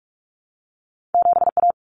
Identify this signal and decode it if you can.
Morse code, heard as a keyed tone, about 46 wpm, 700 Hz. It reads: 7A
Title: Common Cut Numbers - 2-Digit 45wpm